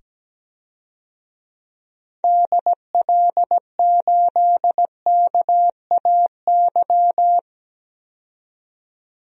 Morse code, heard as a keyed tone, about 17 wpm, 700 Hz. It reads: DL8KAY
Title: DXCC-100 Call Signs - 6 Characters - 17wpm